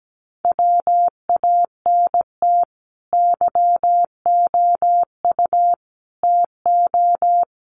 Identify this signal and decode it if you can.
Morse code, heard as a keyed tone, about 17 wpm, 700 Hz. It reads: WANT YOU TO
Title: Sets of 5 Words Encore 17wpm